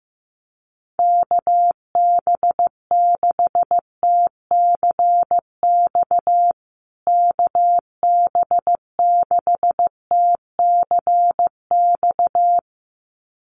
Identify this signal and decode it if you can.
Morse code, heard as a keyed tone, about 15 wpm, 700 Hz. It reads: KB6TCX KB6TCX
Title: USA Call Signs Repeated 2x 15wpm